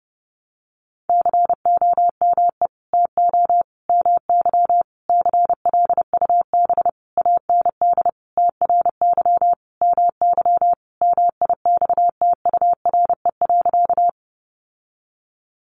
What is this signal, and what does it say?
Morse code, heard as a keyed tone, about 28 wpm, 700 Hz. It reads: COME TO MY CLUB AND TRY MY MIXTURE.
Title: Sentences from Top 1500 Words 30wpm